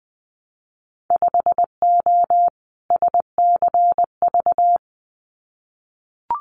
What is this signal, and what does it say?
Morse code, heard as a keyed tone, about 20 wpm, 700 Hz. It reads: 5O SCV
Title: Winter Field Day Exchanges Encore 20wpm